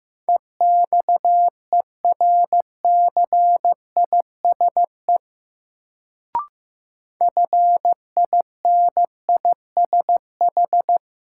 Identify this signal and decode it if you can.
Morse code, heard as a keyed tone, about 15 wpm, 700 Hz. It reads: EXERCISE FINISH
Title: Sentences from Top 1200 Words - New Word Drill Alt - 15wpm